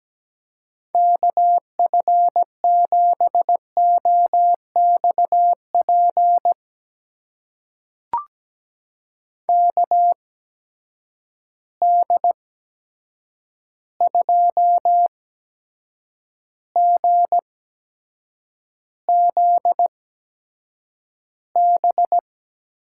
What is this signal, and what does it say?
Morse code, heard as a keyed tone, about 17 wpm, spaced as for 7 wpm, 700 Hz. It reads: KF7OXP KD2GZB
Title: USA Call Signs - 8x Character Spacing - 17wpm